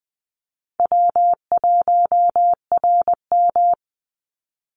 Morse code, heard as a keyed tone, about 20 wpm, 700 Hz. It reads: W1RM
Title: Call Signs - 20wpm